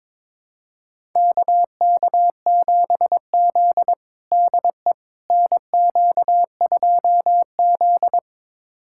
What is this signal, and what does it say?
Morse code, heard as a keyed tone, about 22 wpm, 700 Hz. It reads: KK7Z DE NQ2Z
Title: USA Callsign DE Callsign - 4 Characters - 22wpm